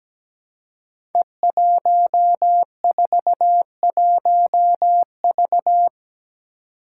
Morse code, heard as a keyed tone, about 17 wpm, 700 Hz. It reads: E141V
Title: Five Challenging Letter-Number Combinations 17wpm